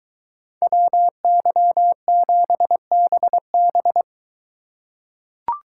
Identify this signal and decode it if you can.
Morse code, heard as a keyed tone, about 23 wpm, 700 Hz. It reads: WY7BB